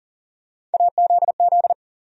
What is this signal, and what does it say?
Morse code, heard as a keyed tone, about 40 wpm, 700 Hz. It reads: AZZ
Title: USA Call Sign Postfixes - 3 Letters 40wpm